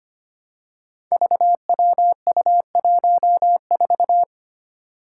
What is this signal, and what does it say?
Morse code, heard as a keyed tone, about 25 wpm, 700 Hz. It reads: VWU14
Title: Five Challenging Letter-Number Combinations 25wpm